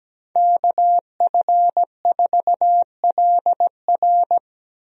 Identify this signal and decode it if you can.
Morse code, heard as a keyed tone, about 17 wpm, 700 Hz. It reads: KF4LR
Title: USA Call Signs - 5 Characters - Repeated 3x - 17wpm